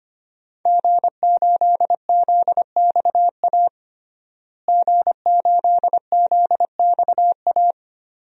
Morse code, heard as a keyed tone, about 25 wpm, 700 Hz. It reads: G8ZXA G8ZXA
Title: DXCC-100 Call Signs - 5 Characters - Repeated 3x - 25wpm